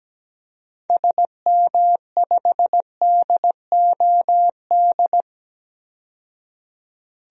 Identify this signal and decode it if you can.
Morse code, heard as a keyed tone, about 17 wpm, 700 Hz. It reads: SM5DOD
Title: DXCC-100 Call Signs - 6 Characters - 17wpm